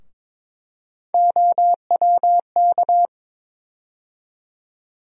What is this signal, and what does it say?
Morse code, heard as a keyed tone, about 21 wpm, 700 Hz. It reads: OWK
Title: USA Call Sign Postfixes - 3 Letters 22wpm